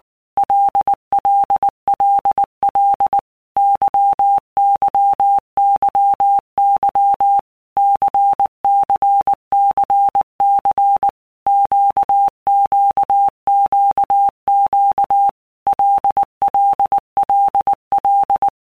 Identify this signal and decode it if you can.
Morse code, heard as a keyed tone, about 19 wpm, 800 Hz. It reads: LLLL YYYY CCCC QQQQ LLLL
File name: Basic Letter Groups By Blocks / Groups@18 / LFCYQ@18